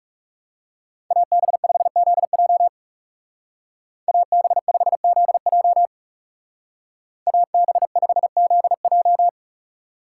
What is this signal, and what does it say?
Morse code, heard as a keyed tone, about 37 wpm, 700 Hz. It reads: AB5ZJ AB5ZJ AB5ZJ
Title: Speed Racing - USA 2x2 Call Signs 30wpm